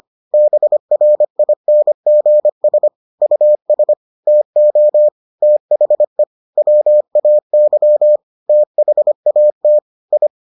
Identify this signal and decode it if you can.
Morse code, heard as a keyed tone, about 25 wpm, 600 Hz. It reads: BRINGS US TO THE WAY THAT I